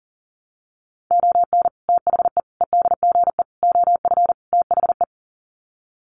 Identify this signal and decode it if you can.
Morse code, heard as a keyed tone, about 40 wpm, 700 Hz. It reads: ON THE EDGE OF THE
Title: Sets of 5 Words 40wpm